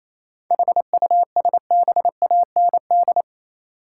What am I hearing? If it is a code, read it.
Morse code, heard as a keyed tone, about 28 wpm, 700 Hz. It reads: HUSBAND
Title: Seven Letter Words - 28wpm